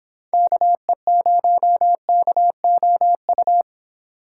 Morse code, heard as a keyed tone, about 29 wpm, 700 Hz. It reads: KE0KOU